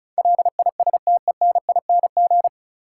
Morse code, heard as a keyed tone, about 36 wpm, 700 Hz. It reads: LISTENING
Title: Sets of 5 Words - New Word Drill - 35wpm